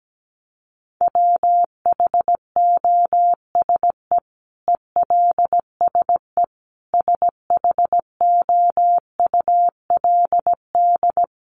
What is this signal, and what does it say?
Morse code, heard as a keyed tone, about 17 wpm, 700 Hz. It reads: WHOSE ELSE SHOULD
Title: Sentences from Top 1200 Words 17wpm